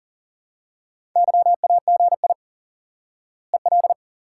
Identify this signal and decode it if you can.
Morse code, heard as a keyed tone, about 40 wpm, 700 Hz. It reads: YAGI EL
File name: IQE Course 3 - Lesson 89 - YAGI - Two Elements - 40wpm